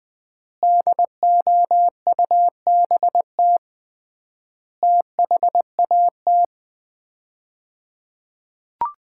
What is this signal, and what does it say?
Morse code, heard as a keyed tone, about 20 wpm, 700 Hz. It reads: DOUBT THAT
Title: 3x Word-Spacing - Sets of 5 Words - 20wpm